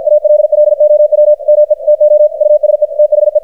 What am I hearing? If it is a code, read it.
Morse code, heard as a keyed tone, about 53 wpm, 600 Hz, with radio noise. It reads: ALLOW ME TO USE THE
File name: archived / CRSnet